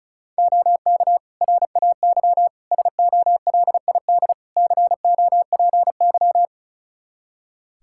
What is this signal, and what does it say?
Morse code, heard as a keyed tone, about 36 wpm, 700 Hz. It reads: OK RAY SOLID COPY
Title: Practice QSO - Exchange 3 - More Info - 35wpm